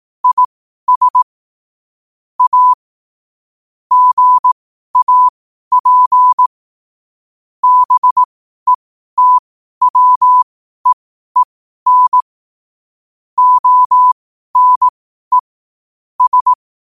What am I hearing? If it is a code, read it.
Morse code, heard as a keyed tone, about 18 wpm, spaced as for 12 wpm, 1000 Hz. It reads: IS A GAP BETWEEN ONE S
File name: qotdfiles / 12wpm